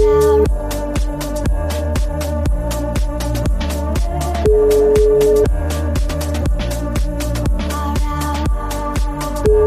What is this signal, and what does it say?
Ringback tone (Europe, 425 Hz), with noise.